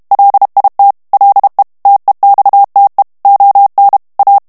Morse code, heard as a keyed tone, about 32 wpm, 800 Hz. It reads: LIT LE TEXTE ON A
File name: morse mp3 / FRANCAIS TEXTE CLAIR 1 VITESSE 30 MOTS MINUTES